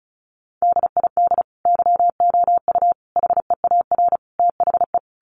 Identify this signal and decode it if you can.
Morse code, heard as a keyed tone, about 34 wpm, 700 Hz. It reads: DID YOU HEAR THE
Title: Sentences from Top 1400 Words 35wpm